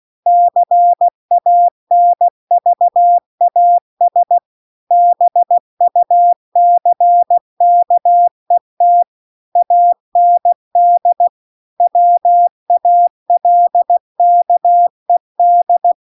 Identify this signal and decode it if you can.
Morse code, heard as a keyed tone, about 16 wpm, 700 Hz. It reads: CANVAS BUCKET AND WALKED